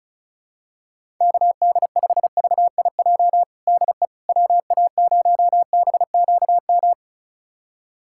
Morse code, heard as a keyed tone, about 36 wpm, 700 Hz. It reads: KD5VIJ DE WA0BQM
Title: USA Callsign DE Callsign - 6 Characters - 35wpm